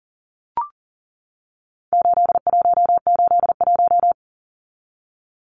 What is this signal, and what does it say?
Morse code, heard as a keyed tone, about 40 wpm, 700 Hz. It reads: 8181
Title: Numbers - 4-Digit 40wpm